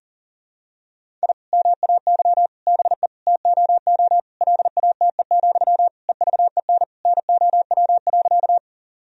Morse code, heard as a keyed tone, about 37 wpm, 700 Hz. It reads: I MAY BE TOO LATE, EVEN NOW.
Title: Sentences from Top 1000 Words - Review Part 1 - 40wpm